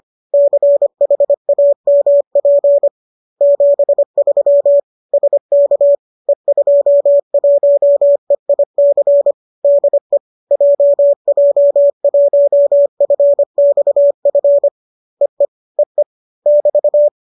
Morse code, heard as a keyed tone, about 25 wpm, 570 Hz. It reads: CHAMP 73 SK E21EIC DE JJ1FXF EE EE <BT>
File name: qso208 25